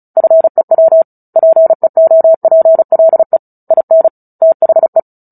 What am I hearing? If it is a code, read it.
Morse code, heard as a keyed tone, about 34 wpm, 650 Hz. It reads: FEW PEOPLE IN THE
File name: Book Two / 09-Chapter 9 - 35 WPM